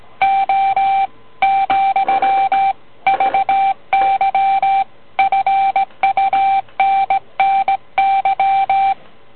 Morse code, heard as a keyed tone, about 17 wpm, 780 Hz. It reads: O , VY FUNNY